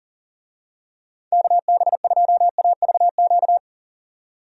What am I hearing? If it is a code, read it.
Morse code, heard as a keyed tone, about 38 wpm, 700 Hz. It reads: KB2AVQ
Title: USA Call Signs - 6 Characters - 40wpm